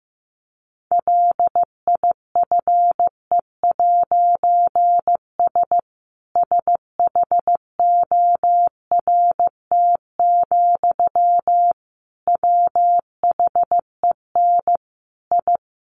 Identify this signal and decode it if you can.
Morse code, heard as a keyed tone, about 15 wpm, 700 Hz. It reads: LIFE'S SHORT, WHEN I
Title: A Sound Of Thunder 15wpm